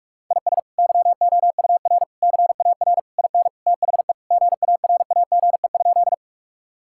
Morse code, heard as a keyed tone, about 42 wpm, 700 Hz. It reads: IS YOUR CAR IN THE GARAGE?
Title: Sentences from Top 1600 Words 45wpm